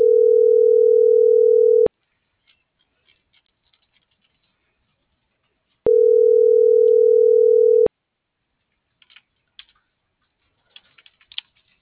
Ringback tone (North America), 440 + 480 Hz.